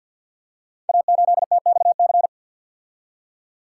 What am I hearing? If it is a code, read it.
Morse code, heard as a keyed tone, about 46 wpm, 690 Hz. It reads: A8TXC
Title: Code Groups - 5 Letter:Numbers - 50wpm